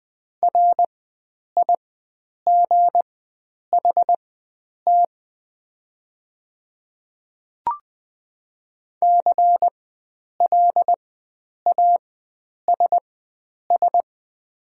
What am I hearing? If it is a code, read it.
Morse code, heard as a keyed tone, about 20 wpm, spaced as for 9 wpm, 690 Hz. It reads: RIGHT CLASS
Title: 4x Character Spacing - Top 500 Words - 20wpm